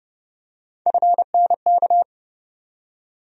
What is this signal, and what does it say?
Morse code, heard as a keyed tone, about 30 wpm, 700 Hz. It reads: FNK
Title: USA Call Sign Postfixes - 3 Letters 30wpm